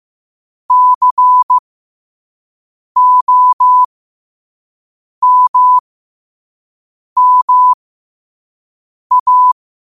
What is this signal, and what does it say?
Morse code, heard as a keyed tone, about 15 wpm, spaced as for 5 wpm, 1000 Hz. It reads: COMMA
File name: qotdfiles / 5wpm